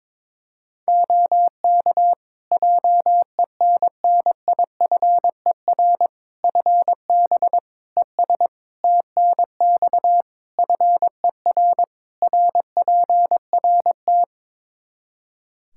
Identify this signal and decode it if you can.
Morse code, heard as a keyed tone, about 22 wpm, 700 Hz. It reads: OK JENNIFER FB ES TNX FER RPRT
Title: Practice QSO - Exchange 2 - Info - 22wpm